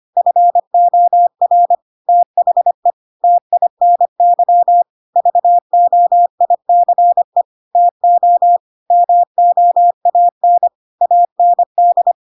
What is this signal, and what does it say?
Morse code, heard as a keyed tone, about 25 wpm, 700 Hz. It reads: FOR THE TINY VOICE TO MOAN AND
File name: Pinocchio / Pinocchio ch1 25 25 ik1hge